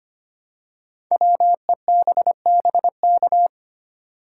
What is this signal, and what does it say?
Morse code, heard as a keyed tone, about 25 wpm, 700 Hz. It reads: WEBBK
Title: Code Groups - 5 Letter:Numbers - 25wpm